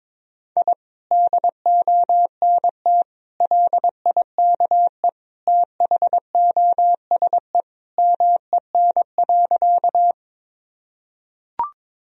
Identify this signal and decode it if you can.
Morse code, heard as a keyed tone, about 22 wpm, 700 Hz. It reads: I DONT LIKE THOSE MEN.
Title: Sentences from Top 1000 Words - Review Part 1 - 22wpm